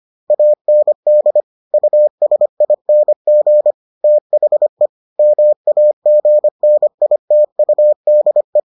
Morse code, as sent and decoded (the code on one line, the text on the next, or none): .- -. -.. / ..- ... .. -. --. / - .... / -- .- --. -. .. - ..- -.. .
AND USING THE MAGNITUDE